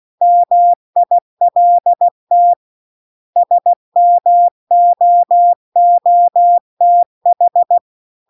Morse code, heard as a keyed tone, about 16 wpm, 700 Hz. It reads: MILT SMOOTH